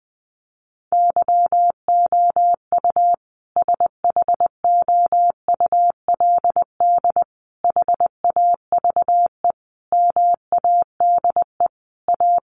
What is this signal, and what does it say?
Morse code, heard as a keyed tone, about 20 wpm, 700 Hz. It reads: YOU SHOULD HAVE MADE A
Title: Sentences from Top 1200 Words 20wpm